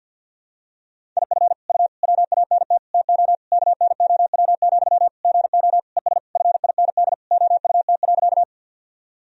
Morse code, heard as a keyed tone, about 46 wpm, 700 Hz. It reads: IF U WANT TO KNOW, GO ES FIND OUT.
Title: Sentences from Top 200 Words and Abbreviations 50wpm